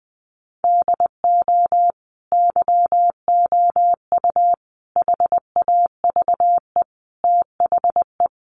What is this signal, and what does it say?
Morse code, heard as a keyed tone, about 20 wpm, 700 Hz. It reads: DO YOU HAVE THE
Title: Sentences from Top 800 Words 20wpm